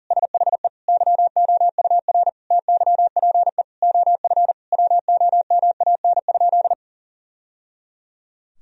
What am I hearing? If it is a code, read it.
Morse code, heard as a keyed tone, about 40 wpm, 700 Hz. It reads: SHE YOUR TYPE OF WOMAN?
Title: Sentences from Top 1000 Words - Review Part 3 - 40wpm